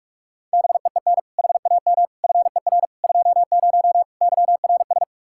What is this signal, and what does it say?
Morse code, heard as a keyed tone, about 46 wpm, 700 Hz. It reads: BEEN HAM FER 20 YRS
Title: Practice QSO - Exchange 3 - More Info - 45wpm